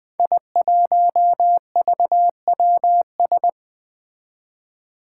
Morse code, heard as a keyed tone, about 20 wpm, 700 Hz. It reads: I1VWS
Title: Five Challenging Letter-Number Combinations 20wpm